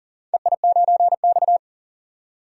Morse code, heard as a keyed tone, about 40 wpm, 700 Hz. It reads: EI9X
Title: DXCC-100 Call Signs - 4 Characters - 40wpm